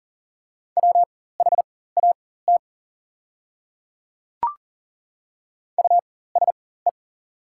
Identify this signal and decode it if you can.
Morse code, heard as a keyed tone, about 40 wpm, spaced as for 19 wpm, 700 Hz. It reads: WHAT USE
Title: 4x Character Spacing - Top 100 Words - 40wpm